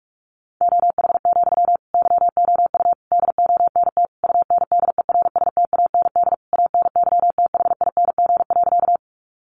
Morse code, heard as a keyed tone, about 44 wpm, 700 Hz. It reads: OH, YOU DONT UNDERSTAND ANYTHING.
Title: Sentences from Top 1000 Words 45wpm